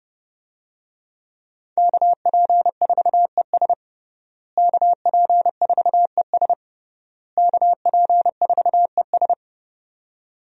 Morse code, heard as a keyed tone, about 30 wpm, 700 Hz. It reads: KP4ES KP4ES KP4ES
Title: USA Call Signs - 5 Characters - Repeated 3x - 30wpm